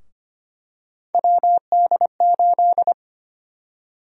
Morse code, heard as a keyed tone, about 25 wpm, 700 Hz. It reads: WD8
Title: USA Call Sign Prefixes - Repeated 2x - 25wpm